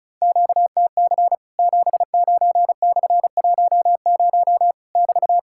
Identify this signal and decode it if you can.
Morse code, heard as a keyed tone, about 36 wpm, 700 Hz. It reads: QTC 79/10 <BT>
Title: Worked All Europe DX Contest - QTC Exchange - 35wpm